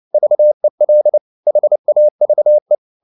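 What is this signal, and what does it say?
Morse code, heard as a keyed tone, about 29 wpm, 600 Hz. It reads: VEL HAVE
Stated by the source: Fast (QRQ) Morse at 29 wpm, from CW med Gnister.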